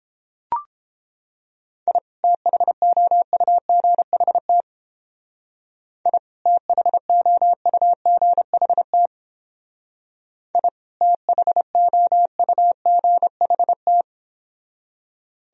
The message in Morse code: I THOUGHT I THOUGHT I THOUGHT

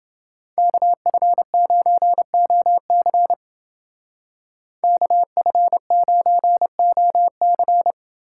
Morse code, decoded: KF9OC KF9OC

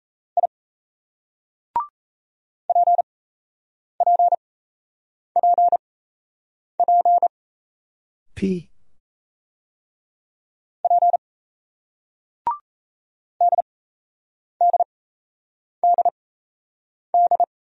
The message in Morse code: I PPPP P DDDD

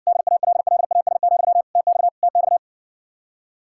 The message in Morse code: DALLAS, TX TX

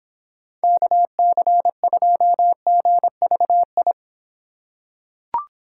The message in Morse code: KC2GVI

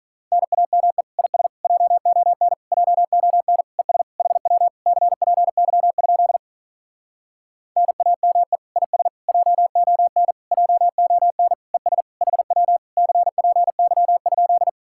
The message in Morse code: NAME IS JON JON ES HW CPY? NAME IS JON JON ES HW CPY?

